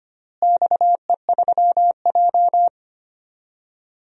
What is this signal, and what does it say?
Morse code, heard as a keyed tone, about 25 wpm, 700 Hz. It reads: XE3J
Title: DXCC-100 Call Signs - 4 Characters - Repeated 2x - 25wpm